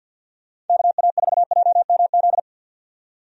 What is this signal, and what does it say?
Morse code, heard as a keyed tone, about 53 wpm, 700 Hz. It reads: KA4JMZ
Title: USA Call Signs - 6 Characters - 50wpm